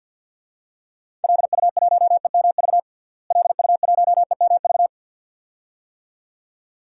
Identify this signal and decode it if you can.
Morse code, heard as a keyed tone, about 52 wpm, 700 Hz. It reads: LU1EMV LU1EMV
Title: DXCC-100 Call Signs - 6 Characters - Repeated 2x - 50wpm